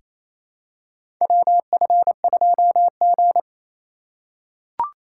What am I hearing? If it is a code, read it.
Morse code, heard as a keyed tone, about 28 wpm, 700 Hz. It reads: WF2G
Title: USA Call Signs - 4 Characters - 3x Character Spacing - 28wpm